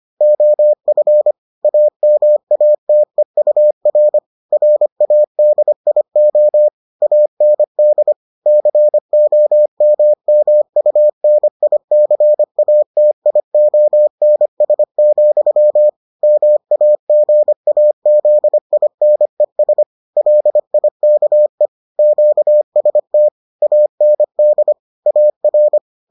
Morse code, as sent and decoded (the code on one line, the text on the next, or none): --- ..-. / .- -- .- - . ..- .-. / .-. .- -.. .. --- / .- -. -.. / -.-. --- -- -- ..- -. .. -.-. .- - .. --- -. ... --..-- / -- .- --. .- --.. .. -. ... / .-.. .. -.- . / --.- ... - / .- -. -.. / .- .-.
OF AMATEUR RADIO AND COMMUNICATIONS, MAGAZINES LIKE QST AND AR